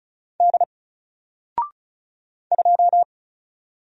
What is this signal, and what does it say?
Morse code, heard as a keyed tone, about 34 wpm, 700 Hz. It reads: D 2
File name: Single-Letter-Number-35wpm